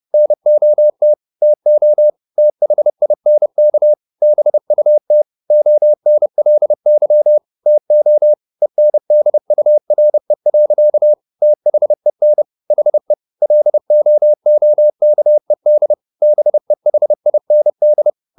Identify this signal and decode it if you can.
Morse code, heard as a keyed tone, about 30 wpm, 600 Hz. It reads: NOT TO THINK BUT ONLY TO ENDURE. THEN HE LOOKED BEHIND
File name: the old man and the sea / the old man and the sea 1 30wpm